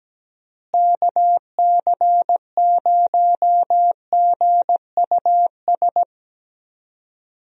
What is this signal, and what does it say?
Morse code, heard as a keyed tone, about 17 wpm, 700 Hz. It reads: KC0GUS
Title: USA Call Signs - 6 Characters - Repeated 2x - 17wpm